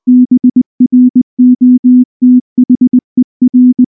Morse code, heard as a keyed tone, about 19 wpm, 250 Hz. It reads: BROTHER
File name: morse / morse-training-20WPM-8